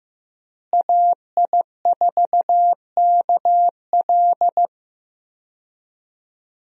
Morse code, heard as a keyed tone, about 15 wpm, 700 Hz. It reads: AI4KL
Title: USA Call Signs - 5 Characters - Repeated 2x - 15wpm